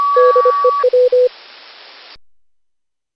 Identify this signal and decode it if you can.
Morse code, heard as a keyed tone, about 25 wpm, 500 Hz. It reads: DEW
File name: cw / Session5 cw training